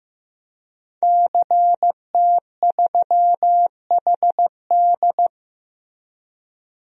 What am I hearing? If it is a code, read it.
Morse code, heard as a keyed tone, about 15 wpm, 700 Hz. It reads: CT3HD
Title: DXCC-100 Call Signs - 5 Characters - Repeated 2x - 15wpm